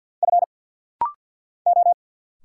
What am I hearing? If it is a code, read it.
Morse code, heard as a keyed tone, about 48 wpm, 690 Hz. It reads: F O
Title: Single Letter-Number Alt - Rapid-Fire 50wpm